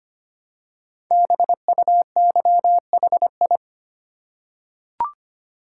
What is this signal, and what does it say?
Morse code, heard as a keyed tone, about 25 wpm, 700 Hz. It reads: BUYHI
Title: Code Groups - 5 Letters - 25wpm